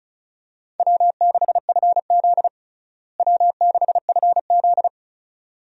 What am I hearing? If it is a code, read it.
Morse code, heard as a keyed tone, about 34 wpm, 700 Hz. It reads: W6FZ W6FZ
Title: USA Call Signs - 4 Characters - Repeated 2x - 35wpm